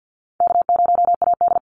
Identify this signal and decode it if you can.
Morse code, heard as a keyed tone, about 50 wpm, 700 Hz. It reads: K0AD
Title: Call Signs - 50wpm